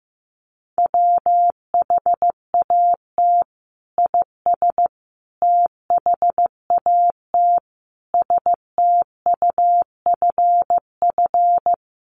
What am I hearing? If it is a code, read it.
Morse code, heard as a keyed tone, about 15 wpm, 700 Hz. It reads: WHAT IS THAT STUFF
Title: Sentences from Top 1400 Words 15wpm